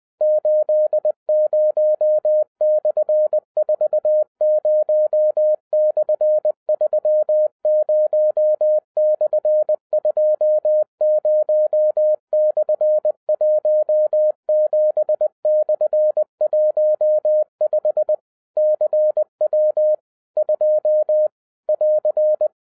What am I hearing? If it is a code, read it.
Morse code, heard as a keyed tone, about 20 wpm, 610 Hz. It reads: 80/40/30/20/17/15 CW 2 <AR>
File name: 2011-07-mp3 / n23 QRP OTAKU 20wpm